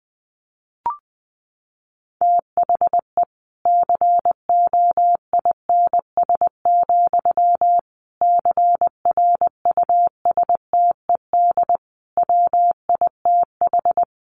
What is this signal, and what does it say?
Morse code, heard as a keyed tone, about 20 wpm, 700 Hz. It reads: THE COINS, CRUSTED WITH
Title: A Sound Of Thunder 20wpm